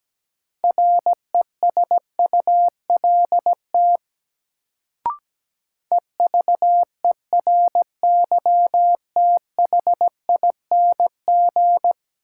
Morse code, heard as a keyed tone, about 17 wpm, 700 Hz. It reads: RESULT EVERYTHING
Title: Sentences from Top 700 Words - New Word Drill - 17wpm